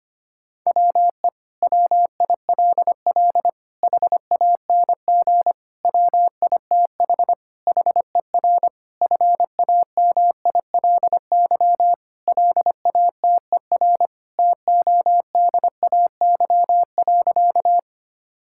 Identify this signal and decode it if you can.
Morse code, heard as a keyed tone, about 25 wpm, 700 Hz. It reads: WE WILL HANG WITH HER FAMILY LATER TODAY.
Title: Sentences from Top 1600 Words 25wpm